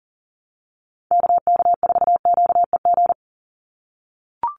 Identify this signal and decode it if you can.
Morse code, heard as a keyed tone, about 40 wpm, 700 Hz. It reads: KK4QEG
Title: USA Call Signs 40wpm